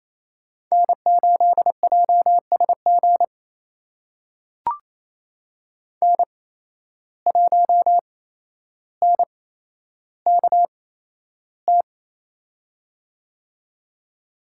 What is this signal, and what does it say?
Morse code, heard as a keyed tone, about 28 wpm, spaced as for 12 wpm, 700 Hz. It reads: N8JSG N1NKT
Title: USA Call Signs - 5 Characters - 8x Character Spacing - 28wpm